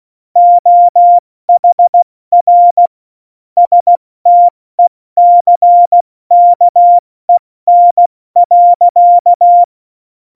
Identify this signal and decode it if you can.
Morse code, heard as a keyed tone, about 16 wpm, spaced as for 14 wpm, 700 Hz. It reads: OHR STECKEN.
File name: Morsekurs / Witze / Witz 16. April 2019, 70 F80